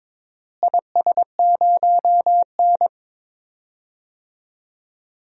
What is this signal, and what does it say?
Morse code, heard as a keyed tone, about 22 wpm, 700 Hz. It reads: IS0N
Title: DXCC-100 Call Signs - 4 Characters - Repeated 2x - 22wpm